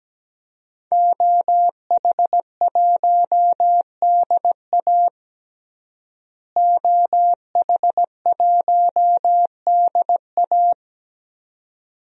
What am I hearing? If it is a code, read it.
Morse code, heard as a keyed tone, about 17 wpm, 700 Hz. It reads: OH1DA OH1DA
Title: DXCC-100 Call Signs - 5 Characters - Repeated 2x - 17wpm